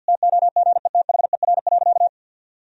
Morse code, heard as a keyed tone, about 50 wpm, 700 Hz. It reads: TOGETHER.
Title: Sentences from Top 1000 Words - Review Part 1 - 50wpm